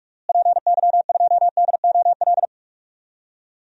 Morse code, heard as a keyed tone, about 46 wpm, 700 Hz. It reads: PY2DOL
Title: DXCC-100 Call Signs - 6 Characters - Repeated 2x - 45wpm